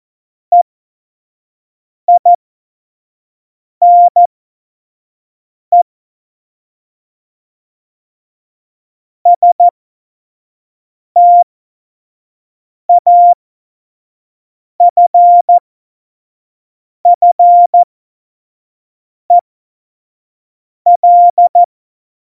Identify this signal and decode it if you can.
Morse code, heard as a keyed tone, about 14 wpm, spaced as for 5 wpm, 700 Hz. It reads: EINE STAFFEL
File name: Morsekurs / Witze / Witz 19. November 2019, 25 F70